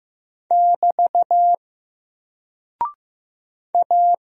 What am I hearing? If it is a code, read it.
Morse code, heard as a keyed tone, about 15 wpm, 700 Hz. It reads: <BT> A
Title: Single Character (Letter:Number:Punctuation:Prosign) Alt 15wpm